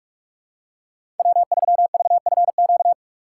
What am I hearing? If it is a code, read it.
Morse code, heard as a keyed tone, about 41 wpm, 700 Hz. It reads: W3VFQ
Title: USA Call Signs - 5 Characters - 4x Character Spacing - 45wpm